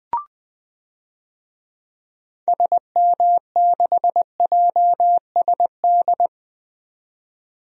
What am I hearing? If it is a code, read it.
Morse code, heard as a keyed tone, about 20 wpm, 700 Hz. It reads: SM6JSD
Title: DXCC-100 Call Signs - 6 Characters - 20wpm